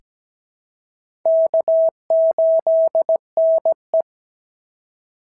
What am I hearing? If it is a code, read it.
Morse code, heard as a keyed tone, about 17 wpm, 650 Hz. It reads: K8NE
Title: SOTA QSO Practice 17wpm